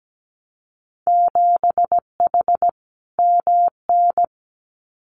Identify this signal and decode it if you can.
Morse code, heard as a keyed tone, about 17 wpm, 700 Hz. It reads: 7H MN
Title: Winter Field Day Exchanges Encore 17wpm